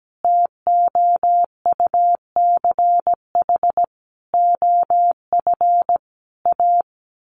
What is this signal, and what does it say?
Morse code, heard as a keyed tone, about 17 wpm, 700 Hz. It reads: TOUCH OF A
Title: A Sound Of Thunder 17wpm